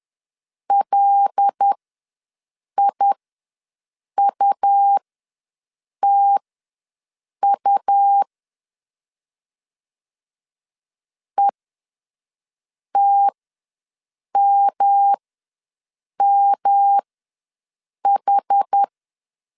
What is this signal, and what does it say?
Morse code, heard as a keyed tone, about 10 wpm, spaced as for 6 wpm, 800 Hz. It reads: LIUTU ETMMH
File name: LEZIONI / 5 LBVC / 5 L ws 10wpm 3636